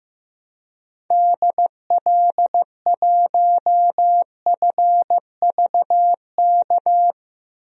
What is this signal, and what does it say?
Morse code, heard as a keyed tone, about 15 wpm, 700 Hz. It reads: DL1FVK
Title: DXCC-100 Call Signs - 6 Characters - 15wpm